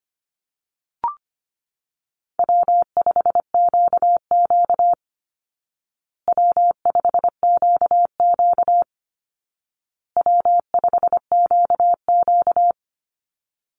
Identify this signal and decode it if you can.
Morse code, heard as a keyed tone, about 25 wpm, 700 Hz. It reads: W5QQ W5QQ W5QQ
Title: Easy Call Signs Repeated 3x - 25wpm